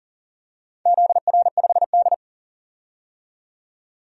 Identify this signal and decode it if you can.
Morse code, heard as a keyed tone, about 40 wpm, 700 Hz. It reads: ZR5D
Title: DXCC-100 Call Signs - 4 Characters - 40wpm